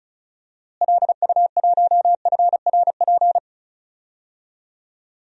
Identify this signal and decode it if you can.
Morse code, heard as a keyed tone, about 36 wpm, 700 Hz. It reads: LU1FRP
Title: DXCC-100 Call Signs - 6 Characters - Repeated 2x - 35wpm